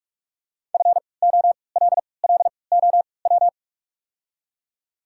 Morse code, heard as a keyed tone, about 44 wpm, spaced as for 25 wpm, 690 Hz. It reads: FOLLOW
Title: 3x Character Spacing - Top 500 Words - 45wpm